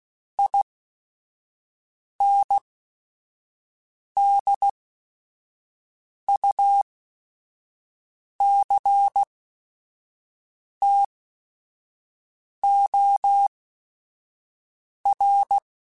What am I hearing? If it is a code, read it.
Morse code, heard as a keyed tone, about 16 wpm, spaced as for 5 wpm, 800 Hz. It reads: INDUCTOR
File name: Technical Subjects / Technical Subjects-mp3s / XL @5